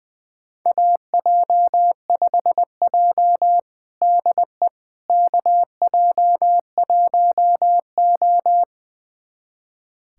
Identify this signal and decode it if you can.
Morse code, heard as a keyed tone, about 21 wpm, 700 Hz. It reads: AJ5J DE KJ1O
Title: USA Callsign DE Callsign - 4 Characters - 20wpm